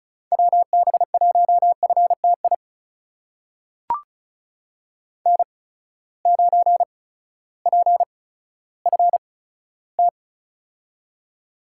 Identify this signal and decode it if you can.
Morse code, heard as a keyed tone, about 36 wpm, spaced as for 16 wpm, 700 Hz. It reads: WB1FTI N9PFT
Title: USA Call Signs - 8x Character Spacing - 35wpm